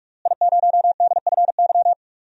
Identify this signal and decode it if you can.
Morse code, heard as a keyed tone, about 48 wpm, 700 Hz. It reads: I0DFY
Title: DXCC-100 Call Signs - 5 Characters - Repeated 3x - 45wpm